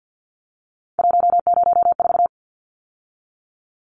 Morse code, heard as a keyed tone, about 49 wpm, 700 Hz. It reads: 104
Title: Numbers - 3-Digit 50wpm